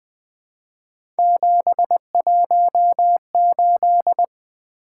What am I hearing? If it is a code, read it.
Morse code, heard as a keyed tone, about 20 wpm, 700 Hz. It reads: 718
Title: Three Letter-Number Combinations - 20wpm